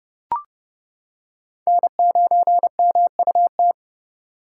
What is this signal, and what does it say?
Morse code, heard as a keyed tone, about 29 wpm, 700 Hz. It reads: N9MUT